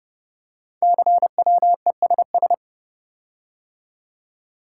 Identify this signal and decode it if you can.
Morse code, heard as a keyed tone, about 28 wpm, 700 Hz. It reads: CWESS
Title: Code Groups - 5 Letters - 30wpm